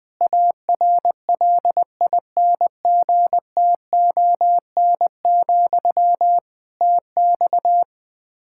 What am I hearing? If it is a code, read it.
Morse code, heard as a keyed tone, about 20 wpm, 700 Hz. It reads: ARLINGTON, TX
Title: Practice QSO - Exchange 1 - Beginning - 20wpm